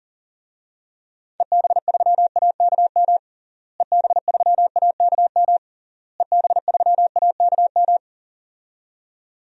Morse code, heard as a keyed tone, about 40 wpm, 700 Hz. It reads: EB3AKM EB3AKM EB3AKM
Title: DXCC-100 Call Signs - 6 Characters - Repeated 3x - 40wpm